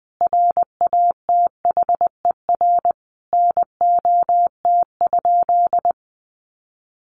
Morse code, heard as a keyed tone, about 20 wpm, 700 Hz. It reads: RATHER NOT?
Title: Sentences from Top 700 Words 20wpm